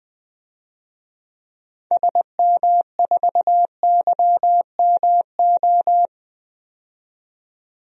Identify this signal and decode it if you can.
Morse code, heard as a keyed tone, about 21 wpm, 700 Hz. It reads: SM4YMO
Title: DXCC-100 Call Signs - 6 Characters - 20wpm